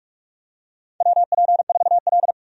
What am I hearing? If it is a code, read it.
Morse code, heard as a keyed tone, about 42 wpm, 700 Hz. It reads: WP4L
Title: USA Call Signs - 4 Characters - 45wpm